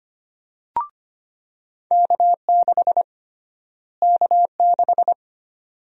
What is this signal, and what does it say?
Morse code, heard as a keyed tone, about 25 wpm, 700 Hz. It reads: K6 K6
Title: USA Call Sign Prefixes - Repeated 2x - 25wpm